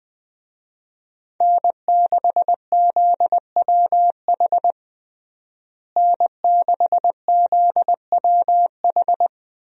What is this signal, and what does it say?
Morse code, heard as a keyed tone, about 20 wpm, 700 Hz. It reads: N6ZWH N6ZWH
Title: USA Call Signs - 5 Characters - Repeated 2x - 20wpm